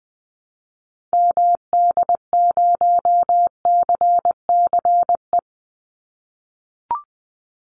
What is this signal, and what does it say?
Morse code, heard as a keyed tone, about 20 wpm, 700 Hz. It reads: MD0CCE
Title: Call Signs - 20wpm